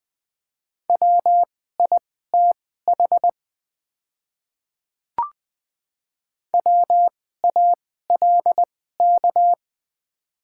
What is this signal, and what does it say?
Morse code, heard as a keyed tone, about 20 wpm, spaced as for 14 wpm, 700 Hz. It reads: WITH WALK
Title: 2x Character Spacing - Top 500 Words - 20wpm